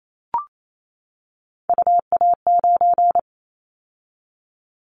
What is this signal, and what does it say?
Morse code, heard as a keyed tone, about 28 wpm, 700 Hz. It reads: UA9
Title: Callsign Prefixes for Top 100 DXCC - 28wpm